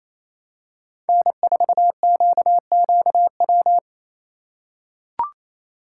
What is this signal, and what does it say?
Morse code, heard as a keyed tone, about 28 wpm, 700 Hz. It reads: N4QQW
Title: USA Call Signs - 5 Characters - 2x Character Spacing - 28wpm